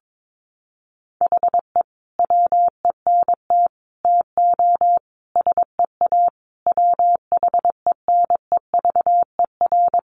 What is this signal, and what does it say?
Morse code, heard as a keyed tone, about 22 wpm, 700 Hz. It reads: HE WENT TO SEA WHENEVER
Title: Sentences from Top 1200 Words 22wpm